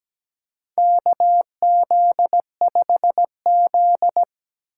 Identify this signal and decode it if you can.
Morse code, heard as a keyed tone, about 17 wpm, 700 Hz. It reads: KZ5Z
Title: K1USN SST QSO Practice - 17wpm